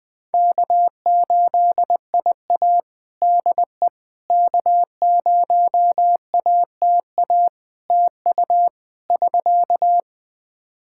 Morse code, heard as a keyed tone, about 21 wpm, 700 Hz. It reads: K8IA DE K0ATA TU <SK>
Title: Practice QSO - Exchange 4 - Ending - 20wpm